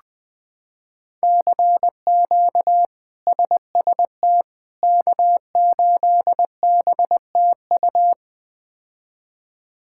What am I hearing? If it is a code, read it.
Morse code, heard as a keyed tone, about 20 wpm, 700 Hz. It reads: CQ SST K8BTU
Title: K1USN SST QSO Practice - 20wpm